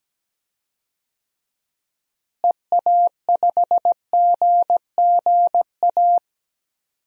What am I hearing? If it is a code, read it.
Morse code, heard as a keyed tone, about 17 wpm, 700 Hz. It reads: EA5GGA
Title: DXCC-100 Call Signs - 6 Characters - Repeated 3x - 17wpm